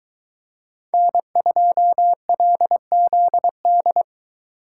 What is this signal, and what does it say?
Morse code, heard as a keyed tone, about 20 wpm, 700 Hz. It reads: N2LZD